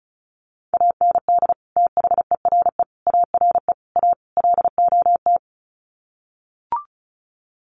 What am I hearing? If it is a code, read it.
Morse code, heard as a keyed tone, about 34 wpm, 700 Hz. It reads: AND THERE ARE A LOT
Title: Sets of 5 Words Encore2 35wpm